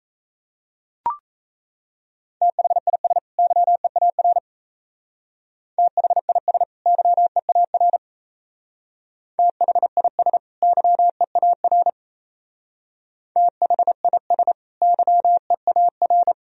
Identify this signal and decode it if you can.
Morse code, heard as a keyed tone, about 35 wpm, 700 Hz. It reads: THIS YEAR THIS YEAR THIS YEAR THIS YEAR
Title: Speed-Racing - Sets of 2 Words 28wpm